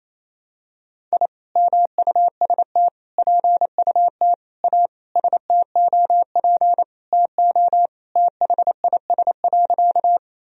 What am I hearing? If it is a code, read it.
Morse code, heard as a keyed tone, about 28 wpm, 700 Hz. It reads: I MUST PUT A STOP TO THIS.
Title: Sentences from Top 1000 Words - Review Part 1 - 28wpm